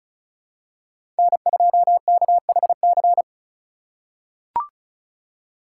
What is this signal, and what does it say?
Morse code, heard as a keyed tone, about 36 wpm, 700 Hz. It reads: N2KHC
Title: USA Call Signs - 5 Characters - 6x Character Spacing - 35wpm